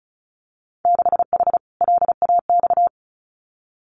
Morse code, heard as a keyed tone, about 36 wpm, 700 Hz. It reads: 6H LAX
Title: Winter Field Day Exchanges Encore 35wpm